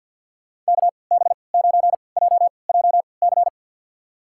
Morse code, heard as a keyed tone, about 52 wpm, spaced as for 29 wpm, 700 Hz. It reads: KB9JJC
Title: USA Call Signs - 6 Characters - 3x Character Spacing - 50wpm